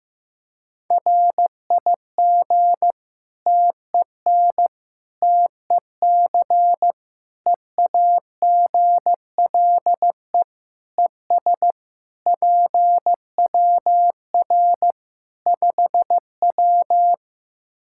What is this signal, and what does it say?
Morse code, heard as a keyed tone, about 15 wpm, 700 Hz. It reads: RIG TEN TEC EAGLE ES PWR 5W
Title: Practice QSO - Exchange 2 - Info - 15wpm